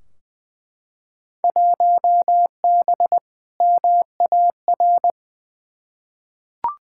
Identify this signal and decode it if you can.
Morse code, heard as a keyed tone, about 21 wpm, 700 Hz. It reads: 1B MAR
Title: Field Day Exchanges Encore - 20wpm